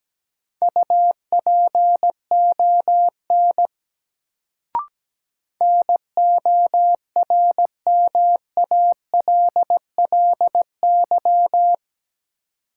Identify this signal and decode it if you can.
Morse code, heard as a keyed tone, about 17 wpm, 700 Hz. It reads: UPON NORMALLY
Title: Sentences from Top 1000 Words - Review Part 6 - New Word Drill - 17wpm